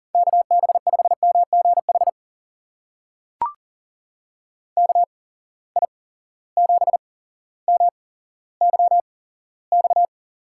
Morse code, heard as a keyed tone, about 40 wpm, spaced as for 17 wpm, 700 Hz. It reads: KB5MGH KI7MYX
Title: USA Call Signs - 6 Characters - 8x Character Spacing - 40wpm